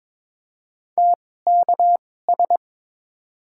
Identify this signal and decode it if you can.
Morse code, heard as a keyed tone, about 21 wpm, spaced as for 16 wpm, 700 Hz. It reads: TKS
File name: Top 90 QSO Elements - 2x Character Spacing - 22wpm